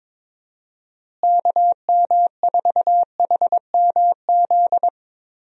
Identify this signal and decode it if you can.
Morse code, heard as a keyed tone, about 22 wpm, 700 Hz. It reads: KM4HMZ
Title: USA Call Signs - 6 Characters - Repeated 3x - 22wpm